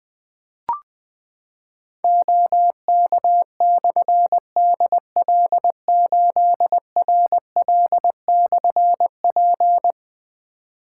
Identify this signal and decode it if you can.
Morse code, heard as a keyed tone, about 21 wpm, 700 Hz. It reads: OK/DL8RL/P
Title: Difficult Call Signs 20wpm